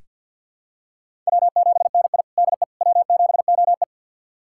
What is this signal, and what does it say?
Morse code, heard as a keyed tone, about 47 wpm, 700 Hz. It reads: W7NI DE W7OE
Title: USA Callsign DE Callsign - 4 Characters - 50wpm